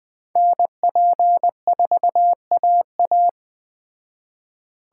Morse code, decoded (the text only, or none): NP4AA